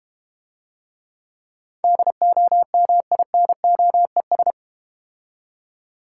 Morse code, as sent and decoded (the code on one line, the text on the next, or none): -.. --- -- .. -. --- . ...
DOMINOES